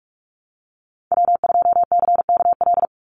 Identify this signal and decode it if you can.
Morse code, heard as a keyed tone, about 44 wpm, 700 Hz. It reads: W2CKL
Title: NA SOTA Activator Callsigns 45wpm